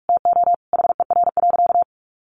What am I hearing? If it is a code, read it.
Morse code, heard as a keyed tone, about 44 wpm, 700 Hz. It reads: TO HER.
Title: Sentences from Top 700 Words 45wpm